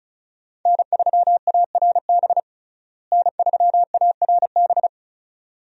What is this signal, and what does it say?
Morse code, heard as a keyed tone, about 34 wpm, 700 Hz. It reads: N3ARB N3ARB
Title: USA Call Signs - 5 Characters - Repeated 2x - 35wpm